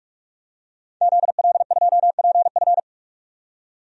Morse code, heard as a keyed tone, about 43 wpm, 700 Hz. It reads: ZL2PF
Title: DXCC-100 Call Signs - 5 Characters - 45wpm